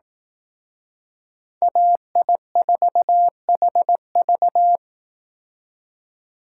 Morse code, heard as a keyed tone, about 18 wpm, 700 Hz. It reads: AI4HV
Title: Speed Racing - USA 2x2 Call Signs 15wpm